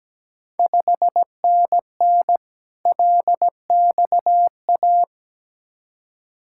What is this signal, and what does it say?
Morse code, heard as a keyed tone, about 17 wpm, 700 Hz. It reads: 5NN LXA
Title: Ontario QSO Party Exchange 17wpm